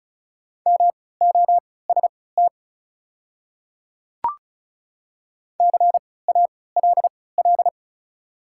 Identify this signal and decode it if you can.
Morse code, heard as a keyed tone, about 36 wpm, spaced as for 20 wpm, 690 Hz. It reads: MOST CALL
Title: 3x Character Spacing - Top 100 Words - 35wpm